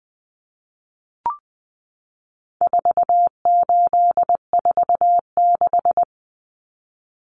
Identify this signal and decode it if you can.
Morse code, heard as a keyed tone, about 20 wpm, 700 Hz. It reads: 4846
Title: Common Cut Numbers - 4-Digit 20wpm